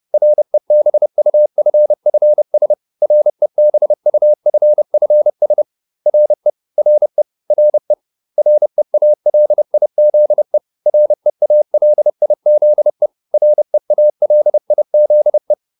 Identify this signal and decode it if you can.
Morse code, heard as a keyed tone, about 30 wpm, 600 Hz. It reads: REBUFFS REBUFFS RE RE RE REALIZE REALIZE REALIZE